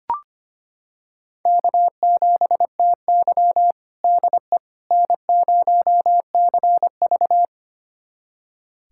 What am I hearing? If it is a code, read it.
Morse code, heard as a keyed tone, about 25 wpm, 700 Hz. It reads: K7TY DE N0CV
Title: USA Callsign DE Callsign - 4 Characters - 25wpm